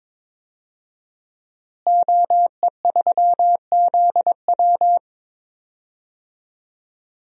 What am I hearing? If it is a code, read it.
Morse code, heard as a keyed tone, about 22 wpm, 700 Hz. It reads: OE3ZW
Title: DXCC-100 Call Signs - 5 Characters - Repeated 3x - 22wpm